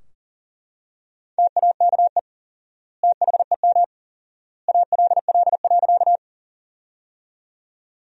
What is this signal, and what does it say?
Morse code, heard as a keyed tone, about 40 wpm, 700 Hz. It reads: TAKE THEM ALL.
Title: 4x Word-Spacing - Sentences from Top 200 Words - 40wpm